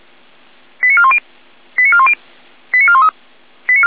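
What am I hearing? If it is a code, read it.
Five-tone selective call (ZVEI), heard as quick runs of five tones, tones of about 70 ms. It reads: 89410 89410 89413
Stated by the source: Received by SDR as ZVEI five-tone selective call 89084.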